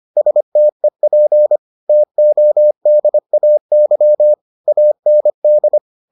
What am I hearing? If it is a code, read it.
Morse code, heard as a keyed tone, about 25 wpm, 600 Hz. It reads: STEP TODAY AND